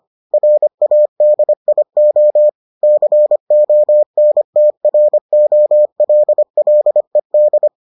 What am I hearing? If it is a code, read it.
Morse code, heard as a keyed tone, about 25 wpm, 600 Hz. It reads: RADIO CONTROLLED